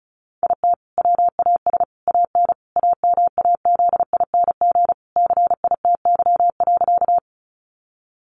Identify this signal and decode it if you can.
Morse code, heard as a keyed tone, about 34 wpm, 700 Hz. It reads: IT WAS AN AMAZING CITY.
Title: Sentences from Top 1500 Words 35wpm